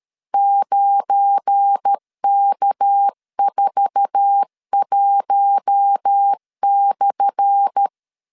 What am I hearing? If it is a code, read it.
Morse code, heard as a keyed tone, about 13 wpm, 800 Hz. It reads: 9K41/
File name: LEZIONI / 10 SIGNS / 10A SLASH 12wpm 2314